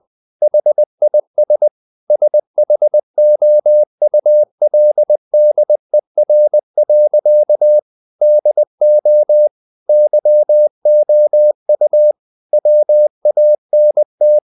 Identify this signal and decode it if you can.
Morse code, heard as a keyed tone, about 20 wpm, 600 Hz. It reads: HIS SHOULDER. DO YOU WANT
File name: the old man and the sea / the old man and the sea part 1 20wpm